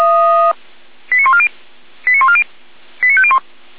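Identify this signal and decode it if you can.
Five-tone selective call (ZVEI), heard as quick runs of five tones, tones of about 70 ms. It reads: 89140 89140 89501
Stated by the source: ZVEI five-tone call 89055 received on SDR, logged by feuerwehr-orlamuende fire brigade.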